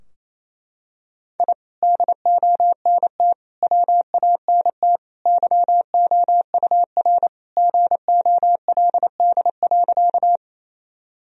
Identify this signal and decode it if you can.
Morse code, heard as a keyed tone, about 28 wpm, 700 Hz. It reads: I DONT WANT YOUR GOLD.
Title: Sentences from Top 1000 Words - Review Part 3 - 28wpm